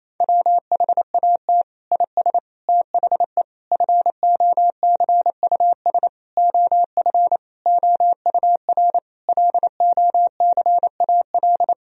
Morse code, heard as a keyed tone, about 27 wpm, 700 Hz. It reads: WHAT IS THE FOCUS OF OUR LOCAL
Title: Sentences from Top 1000 Words - Review Part 4 - 28wpm